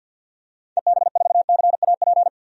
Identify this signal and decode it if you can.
Morse code, heard as a keyed tone, about 52 wpm, 700 Hz. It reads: EB4CAP
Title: DXCC-100 Call Signs - 6 Characters - 50wpm